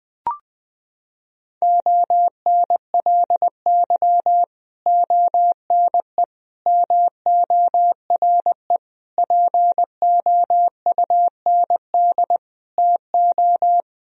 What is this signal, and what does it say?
Morse code, heard as a keyed tone, about 20 wpm, 700 Hz. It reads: ONLY ONE MORE POUND TO
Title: Sentences from Top 1000 Words - Review Part 2 - 20wpm